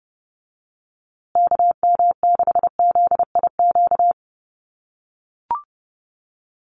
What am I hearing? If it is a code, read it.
Morse code, heard as a keyed tone, about 30 wpm, 700 Hz. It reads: KM6ZIQ
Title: USA Call Signs Repeated 2x 30wpm